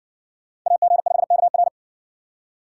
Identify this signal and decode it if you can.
Morse code, heard as a keyed tone, about 57 wpm, 700 Hz. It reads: AG5KR